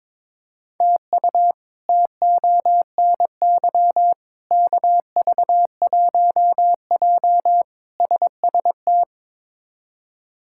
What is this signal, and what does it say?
Morse code, heard as a keyed tone, about 22 wpm, 700 Hz. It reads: TU TONY KV1J SST
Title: K1USN SST QSO Practice - 22wpm